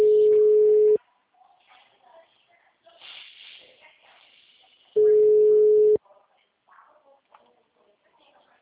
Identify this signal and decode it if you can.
Ringback tone (Europe, 425 Hz), with speech.